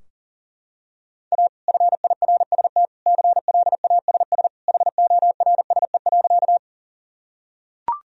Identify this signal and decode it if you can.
Morse code, heard as a keyed tone, about 37 wpm, 700 Hz. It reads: A FIRST CLASS HORSE.
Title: Sentences from Top 1000 Words - Review Part 2 - 40wpm